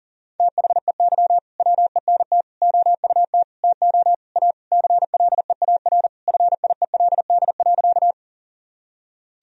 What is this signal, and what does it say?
Morse code, heard as a keyed tone, about 37 wpm, 700 Hz. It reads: THEY WENT OUT TO A CLEAR FIELD.
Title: Sentences from Top 1000 Words - Review Part 2 - 40wpm